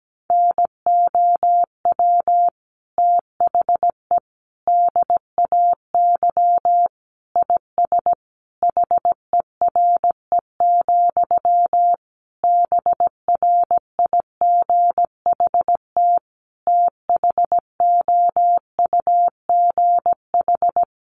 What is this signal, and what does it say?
Morse code, heard as a keyed tone, about 17 wpm, 700 Hz. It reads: NOW THE DAY IS HERE, BRIGHT THOUGH
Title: Sentences from Top 1000 Words 17wpm